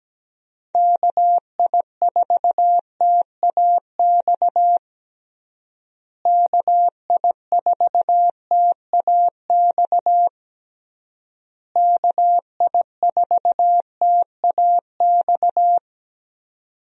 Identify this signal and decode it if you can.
Morse code, heard as a keyed tone, about 17 wpm, 700 Hz. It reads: KI4TAX KI4TAX KI4TAX
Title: USA Call Signs - 6 Characters - Repeated 3x - 17wpm